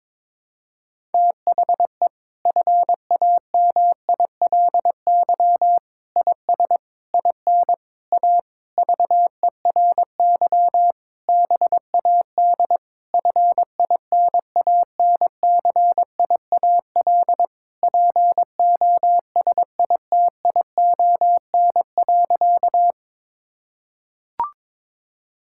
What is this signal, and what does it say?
Morse code, heard as a keyed tone, about 22 wpm, 700 Hz. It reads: THE FAMILY IS IN A VERY BAD FINANCIAL POSITION.
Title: Sentences from Top 1000 Words - Review Part 5 - 22wpm